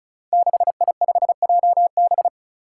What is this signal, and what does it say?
Morse code, heard as a keyed tone, about 36 wpm, 700 Hz. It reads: 6I5JB
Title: Five Challenging Letter-Number Combinations 35wpm